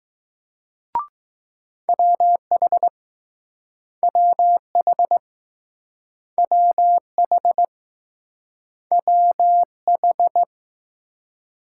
Morse code, heard as a keyed tone, about 20 wpm, 700 Hz. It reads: WH WH WH W H